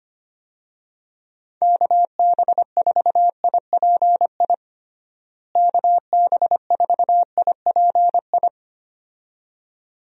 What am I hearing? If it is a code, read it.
Morse code, heard as a keyed tone, about 25 wpm, 700 Hz. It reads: KB4IPI KB4IPI
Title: USA Call Signs - 6 Characters - Repeated 2x - 25wpm